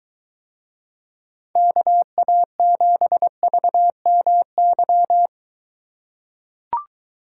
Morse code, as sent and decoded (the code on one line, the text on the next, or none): -.- .- --... ...- -- -.--
KA7VMY